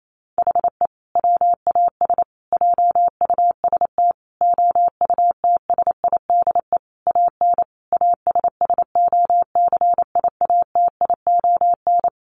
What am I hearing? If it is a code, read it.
Morse code, heard as a keyed tone, about 28 wpm, 700 Hz. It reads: HE WAS JUST OUTSIDE AN ASSOCIATION
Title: Sentences from Top 1100 Words 28wpm